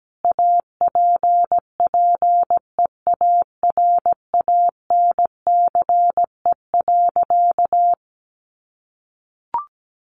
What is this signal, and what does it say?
Morse code, heard as a keyed tone, about 17 wpm, 700 Hz. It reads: APPEARANCE.
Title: Sentences from Top 1100 Words 17wpm